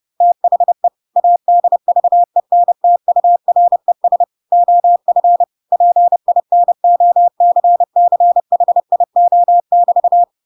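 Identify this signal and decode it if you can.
Morse code, heard as a keyed tone, about 30 wpm, 700 Hz. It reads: THE ADVENTURES OF PINOCCHIO<BT>
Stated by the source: Chapter 2 of The Adventures of Pinocchio read as a book in Morse code (CW) at 30 wpm, by ik1hge.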